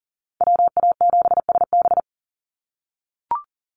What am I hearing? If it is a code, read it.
Morse code, heard as a keyed tone, about 40 wpm, 700 Hz. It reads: WA7SB
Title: NA SOTA Chaser Callsigns 40wpm